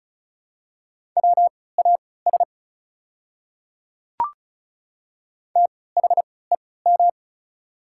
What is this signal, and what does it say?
Morse code, heard as a keyed tone, about 36 wpm, spaced as for 20 wpm, 700 Hz. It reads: WAS THEM
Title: 3x Character Spacing - Top 200 Words - 35wpm